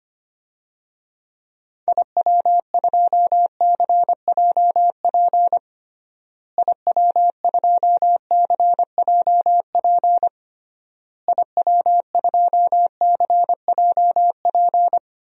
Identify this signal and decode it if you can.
Morse code, heard as a keyed tone, about 25 wpm, 700 Hz. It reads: IW2CJP IW2CJP IW2CJP
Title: DXCC-100 Call Signs - 6 Characters - Repeated 3x - 25wpm